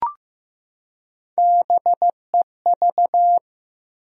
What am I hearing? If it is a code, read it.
Morse code, heard as a keyed tone, about 15 wpm, 700 Hz. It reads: BEV
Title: USA Common Names - Max 4 Letters - 15wpm